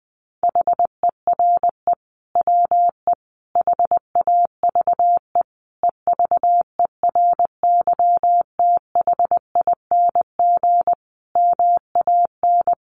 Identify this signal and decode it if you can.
Morse code, heard as a keyed tone, about 19 wpm, 700 Hz. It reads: HERE WE HAVE EVERYTHING MAN
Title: Sentences from Top 1100 Words 20wpm